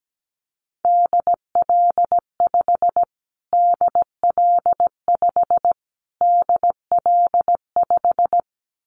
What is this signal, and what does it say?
Morse code, heard as a keyed tone, about 17 wpm, 700 Hz. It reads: DL5 DL5 DL5
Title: Callsign Prefixes for Top 25 DXCC - Repeated 3x - 17wpm